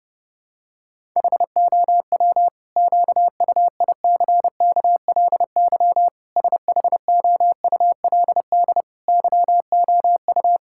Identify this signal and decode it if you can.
Morse code, heard as a keyed tone, about 32 wpm, 700 Hz. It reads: HOW QUICKLY SHOULD YOU
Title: Sentences from Top 1000 Words - Review Part 5 - 30wpm